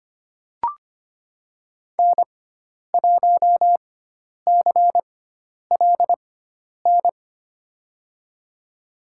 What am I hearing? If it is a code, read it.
Morse code, heard as a keyed tone, about 25 wpm, spaced as for 10 wpm, 700 Hz. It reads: N1CLN
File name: Lesson 16g - Letter c - Call Signs 25wpm